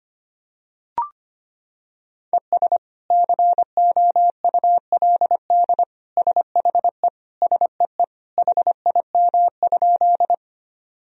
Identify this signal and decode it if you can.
Morse code, heard as a keyed tone, about 25 wpm, 700 Hz. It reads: ES COULD SHE SEE HIM?
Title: Sentences from Top 100 Words and Abbreviations 25wpm